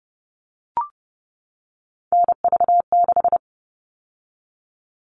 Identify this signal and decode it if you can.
Morse code, heard as a keyed tone, about 30 wpm, 700 Hz. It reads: NV6
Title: All Cut Numbers - 3-Digit 30wpm